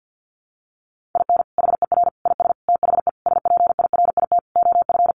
Morse code, heard as a keyed tone, about 50 wpm, 700 Hz. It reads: IN HER IS THE SPIRIT OF
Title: Sentences from Top 1400 Words 50wpm